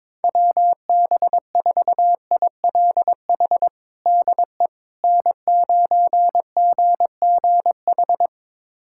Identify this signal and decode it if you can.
Morse code, heard as a keyed tone, about 22 wpm, 700 Hz. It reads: WB4ILH DE N9GGH
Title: USA Callsign DE Callsign 22wpm